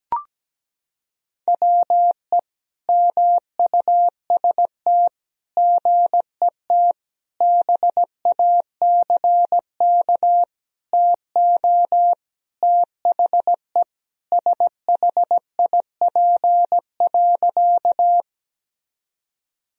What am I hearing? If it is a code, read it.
Morse code, heard as a keyed tone, about 17 wpm, 700 Hz. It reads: WE MUST GET BACK TO THE SHIP.
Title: Sentences from Top 1000 Words - Review Part 2 - 17wpm